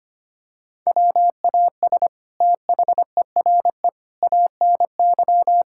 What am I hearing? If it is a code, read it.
Morse code, heard as a keyed tone, about 25 wpm, 700 Hz. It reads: WAS THERE ANY
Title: Sentences from Top 1600 Words 25wpm